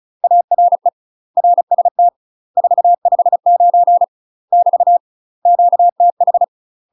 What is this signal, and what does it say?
Morse code, heard as a keyed tone, about 36 wpm, 700 Hz. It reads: ARE RST 459 <BT> QTH
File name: UK MASTER QSOs TRAINING / 35WPM / UK MASTER QSOs 0026